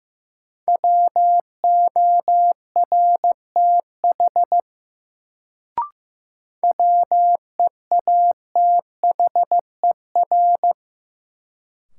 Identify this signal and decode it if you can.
Morse code, heard as a keyed tone, about 15 wpm, 700 Hz. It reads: WORTH WEATHER
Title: Sentences from Top 800 Words - New Word Drill - 15wpm